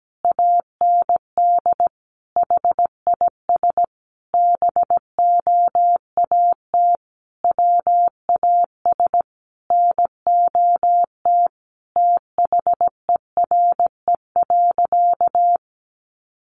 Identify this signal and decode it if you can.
Morse code, heard as a keyed tone, about 17 wpm, 700 Hz. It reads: AND HIS BOAT WAS NOT THERE.
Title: Sentences from Top 1400 Words 17wpm